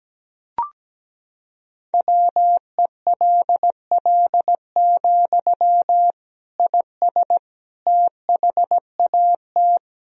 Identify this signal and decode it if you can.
Morse code, heard as a keyed tone, about 17 wpm, 700 Hz. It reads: WELL, IS THAT
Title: Sentences from Top 1000 Words - Review Part 1 - 17wpm